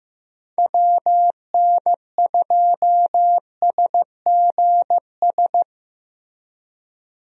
Morse code, heard as a keyed tone, about 15 wpm, 700 Hz. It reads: WN2SGS
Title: USA Call Signs - 6 Characters - Repeated 3x - 15wpm